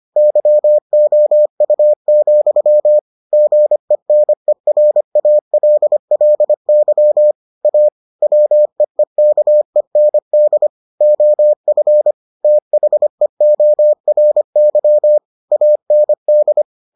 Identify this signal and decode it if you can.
Morse code, heard as a keyed tone, about 25 wpm, 600 Hz. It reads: YOU, GENERALLY A WEEKEND OF THEORY AND